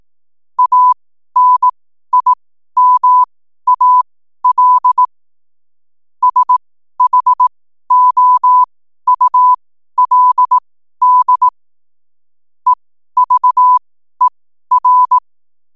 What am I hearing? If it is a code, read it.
Morse code, heard as a keyed tone, about 18 wpm, spaced as for 12 wpm, 1000 Hz. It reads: ANIMAL SHOULD EVER
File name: n7net / 12wpm